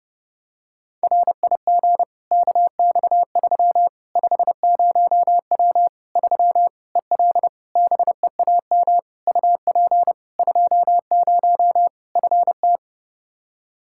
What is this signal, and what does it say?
Morse code, heard as a keyed tone, about 30 wpm, 700 Hz. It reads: RIG KX3 50W 3 EL BEAM UP 20 FT
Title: Practice QSO - Exchange 2 - Info - 30wpm